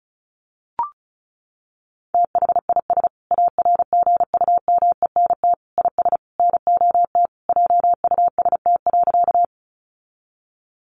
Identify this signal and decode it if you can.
Morse code, heard as a keyed tone, about 34 wpm, 700 Hz. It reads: THIS ARGUMENT IS NOT JUST.
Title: Sentences from Top 1400 Words 35wpm